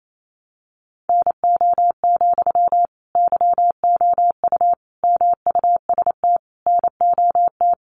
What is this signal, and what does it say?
Morse code, heard as a keyed tone, about 28 wpm, 700 Hz. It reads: NO, YOU MUST NOT
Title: Sentences from Top 1200 Words 28wpm